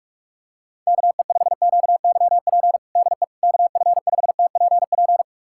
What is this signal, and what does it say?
Morse code, heard as a keyed tone, about 41 wpm, 700 Hz. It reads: KE5QYP DE KF5TPP
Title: USA Callsign DE Callsign - 6 Characters - 45wpm